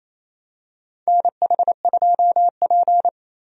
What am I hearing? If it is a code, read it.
Morse code, heard as a keyed tone, about 28 wpm, 700 Hz. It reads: NH2P
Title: USA Call Signs - 4 Characters - 28wpm